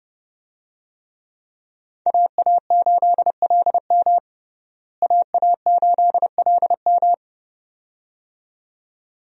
Morse code, heard as a keyed tone, about 32 wpm, 700 Hz. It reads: AA8LM AA8LM
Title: USA Call Signs - 5 Characters - Repeated 2x - 30wpm